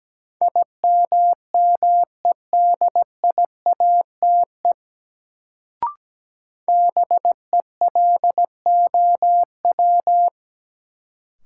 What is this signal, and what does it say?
Morse code, heard as a keyed tone, about 17 wpm, 700 Hz. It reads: IMMEDIATE BELOW
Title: Sentences from Top 1200 Words - New Word Drill Alt - 17wpm